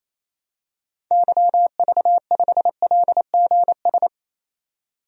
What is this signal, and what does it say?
Morse code, heard as a keyed tone, about 28 wpm, 700 Hz. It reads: YV5LGS
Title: DXCC-100 Call Signs - 6 Characters - 28wpm